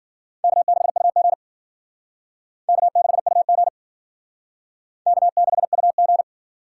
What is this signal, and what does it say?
Morse code, heard as a keyed tone, about 50 wpm, 700 Hz. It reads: K6UG K6UG K6UG
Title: Speed Racing - USA Call Signs 40wpm